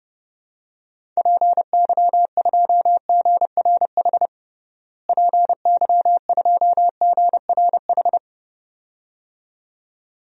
Morse code, heard as a keyed tone, about 30 wpm, 700 Hz. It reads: PY2GRH PY2GRH
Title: DXCC-100 Call Signs - 6 Characters - Repeated 2x - 30wpm